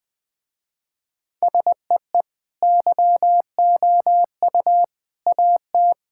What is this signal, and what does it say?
Morse code, heard as a keyed tone, about 20 wpm, 700 Hz. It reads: SEE YOU AT
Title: Sentences with 3-Word Phrases - Spoken Beginning - 20wpm